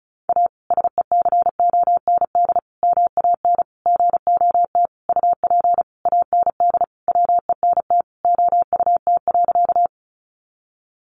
Morse code, heard as a keyed tone, about 34 wpm, 700 Hz. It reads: A SECOND MAN GOT UP AND WENT OUT.
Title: Sentences from Top 300 Words 35wpm